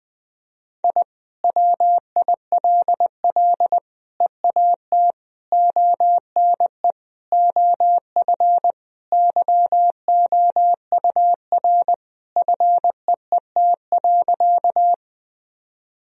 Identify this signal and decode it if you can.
Morse code, heard as a keyed tone, about 21 wpm, 700 Hz. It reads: I WILL EAT ONE OF YOUR FEET.
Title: Sentences from Top 1000 Words - Review Part 1 - 20wpm